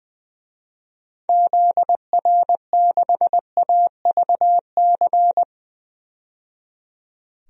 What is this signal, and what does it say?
Morse code, heard as a keyed tone, about 20 wpm, 700 Hz. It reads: ZR6AVC
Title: DXCC-100 Call Signs - 6 Characters - 20wpm